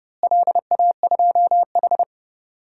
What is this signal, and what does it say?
Morse code, heard as a keyed tone, about 30 wpm, 700 Hz. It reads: LA2H
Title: DXCC-100 Call Signs - 4 Characters - Repeated 2x - 30wpm